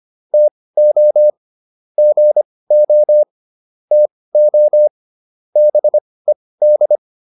Morse code, heard as a keyed tone, about 25 wpm, spaced as for 18 wpm, 600 Hz. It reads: TO GO TO BED